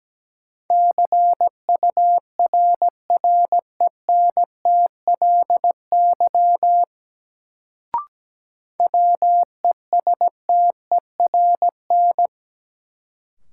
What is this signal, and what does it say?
Morse code, heard as a keyed tone, about 17 wpm, 700 Hz. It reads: CURRENTLY WESTERN
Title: Sentences from Top 1000 Words - Review Part 7 - New Word Drill - 17wpm